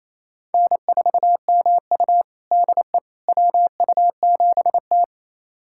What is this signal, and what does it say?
Morse code, heard as a keyed tone, about 28 wpm, 700 Hz. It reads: N4MU DE WU7T
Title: USA Callsign DE Callsign - 4 Characters - 28wpm